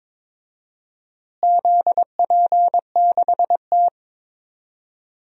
Morse code, heard as a keyed tone, about 22 wpm, 700 Hz. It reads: ZP6T
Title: DXCC-100 Call Signs - 4 Characters - Repeated 2x - 22wpm